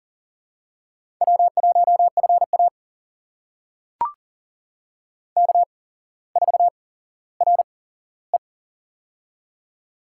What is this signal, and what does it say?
Morse code, heard as a keyed tone, about 40 wpm, spaced as for 17 wpm, 700 Hz. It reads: W1FA K4RE
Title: USA Call Signs - 4 Characters - 8x Character Spacing - 40wpm